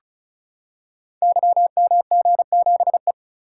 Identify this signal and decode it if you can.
Morse code, heard as a keyed tone, about 35 wpm, 690 Hz. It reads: YMG7E
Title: Code Groups - 5 Letter:Numbers - 35wpm